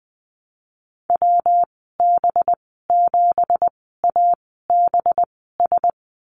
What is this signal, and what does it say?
Morse code, heard as a keyed tone, about 20 wpm, spaced as for 15 wpm, 700 Hz. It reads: WB7ABS
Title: 193 Lesson 31g - Letter v - Call Signs 20wpm